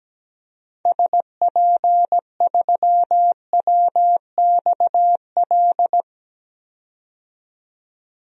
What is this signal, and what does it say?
Morse code, heard as a keyed tone, about 17 wpm, 700 Hz. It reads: SP3WXL
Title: DXCC-100 Call Signs - 6 Characters - 17wpm